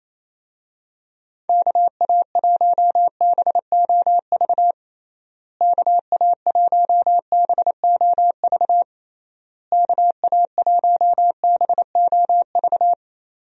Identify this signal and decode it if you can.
Morse code, heard as a keyed tone, about 27 wpm, 700 Hz. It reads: KA1BOV KA1BOV KA1BOV
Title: USA Call Signs - 6 Characters - Repeated 3x - 28wpm